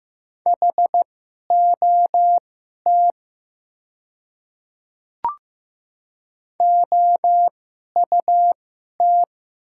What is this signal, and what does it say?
Morse code, heard as a keyed tone, about 15 wpm, spaced as for 11 wpm, 700 Hz. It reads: HOT OUT
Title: 2x Character Spacing - Top 100 Words - 15wpm